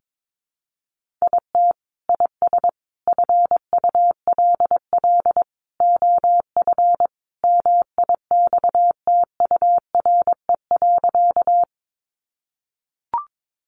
Morse code, heard as a keyed tone, about 22 wpm, 700 Hz. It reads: IT IS FULL OF MIXTURE.
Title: Sentences from Top 1500 Words 22wpm